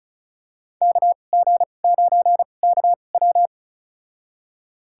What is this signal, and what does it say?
Morse code, heard as a keyed tone, about 36 wpm, spaced as for 25 wpm, 700 Hz. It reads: KG9KW
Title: USA Call Signs - 5 Characters - 2x Character Spacing - 35wpm